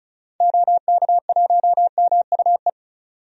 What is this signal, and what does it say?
Morse code, heard as a keyed tone, about 36 wpm, 700 Hz. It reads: OK1MUE
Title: DXCC-100 Call Signs - 6 Characters - 35wpm